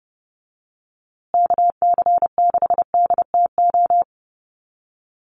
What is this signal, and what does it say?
Morse code, heard as a keyed tone, about 30 wpm, 700 Hz. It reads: KC6DTO
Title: USA Call Signs Repeated 3x 30wpm